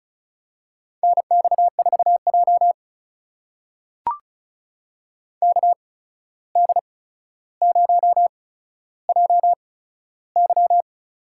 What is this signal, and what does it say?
Morse code, heard as a keyed tone, about 36 wpm, spaced as for 14 wpm, 700 Hz. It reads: NX4J KD0JY
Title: USA Call Signs - 8x Character Spacing - 35wpm